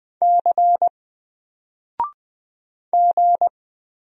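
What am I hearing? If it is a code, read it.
Morse code, heard as a keyed tone, about 20 wpm, 700 Hz. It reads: C G
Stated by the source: Single letter and number Morse drill at 20 wpm, from N4LCD.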